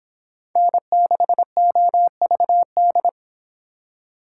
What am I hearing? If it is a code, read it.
Morse code, heard as a keyed tone, about 26 wpm, 700 Hz. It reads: N6OVD